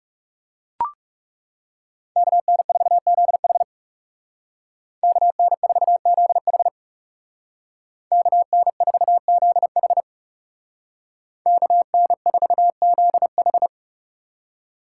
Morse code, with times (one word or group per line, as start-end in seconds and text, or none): 2.2-3.6 KN4ZH
5.0-6.7 KN4ZH
8.1-10.0 KN4ZH
11.5-13.7 KN4ZH